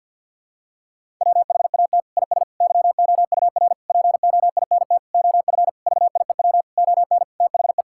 Morse code, heard as a keyed tone, about 52 wpm, 700 Hz. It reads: WHAT IS YOUR POINT OF VIEW ON THE
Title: Sentences from Top 1000 Words - Review Part 6 - 50wpm